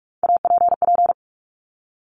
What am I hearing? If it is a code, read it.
Morse code, heard as a keyed tone, about 46 wpm, 700 Hz. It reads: APP
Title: Top Three Letter Combinations in Words 45wpm